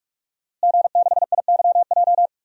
Morse code, heard as a keyed tone, about 43 wpm, 700 Hz. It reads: G6IYJ
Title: DXCC-100 Call Signs - 5 Characters - Repeated 3x - 45wpm